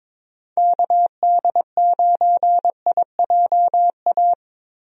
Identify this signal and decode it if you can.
Morse code, heard as a keyed tone, about 22 wpm, 700 Hz. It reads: KD9IJA
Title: USA Call Signs - 6 Characters - Repeated 3x - 22wpm